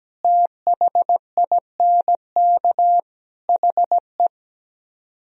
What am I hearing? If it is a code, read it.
Morse code, heard as a keyed tone, about 17 wpm, 700 Hz. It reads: THINK HE
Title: Speed-Racing - Sets of 4 Words 17wpm